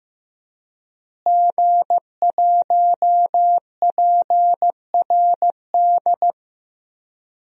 Morse code, heard as a keyed tone, about 16 wpm, 700 Hz. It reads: G1PRD
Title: DXCC-100 Call Signs - 5 Characters - Repeated 2x - 15wpm